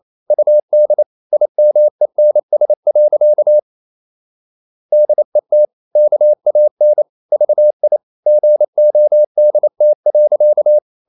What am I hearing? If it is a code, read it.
Morse code, heard as a keyed tone, about 28 wpm, 600 Hz. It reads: UD IMENS. DET KAN VI GODT.
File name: Ferie QRQ 28wpm